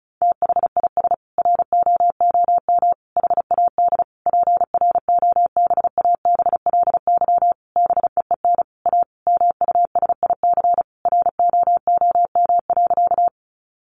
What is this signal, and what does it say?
Morse code, heard as a keyed tone, about 34 wpm, 700 Hz. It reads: THIS ROOM HAD PROBABLY BEEN A MUSIC ROOM.
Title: Sentences from Top 700 Words 35wpm